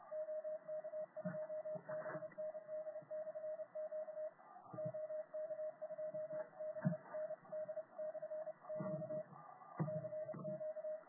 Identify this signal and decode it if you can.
Morse code, heard as a keyed tone, about 29 wpm, 620 Hz. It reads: OK2BND/Q OK2BND/Q CQ